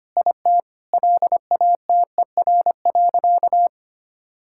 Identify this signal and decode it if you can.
Morse code, heard as a keyed tone, about 25 wpm, 700 Hz. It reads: IT LATER.
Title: Sentences from Top 1000 Words - Review Part 5 - 25wpm